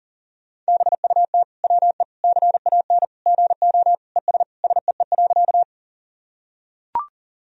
Morse code, heard as a keyed tone, about 40 wpm, 700 Hz. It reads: BUT WE CAN GO ES SEE.
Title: Sentences from Top 100 Words and Abbreviations 40wpm